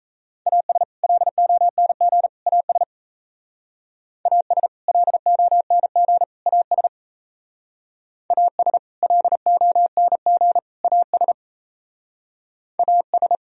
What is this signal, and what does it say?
Morse code, heard as a keyed tone, about 34 wpm, 700 Hz. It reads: AS LONG AS AS LONG AS AS LONG AS AS
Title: Speed-Racing - Sets of 3 Words 28wpm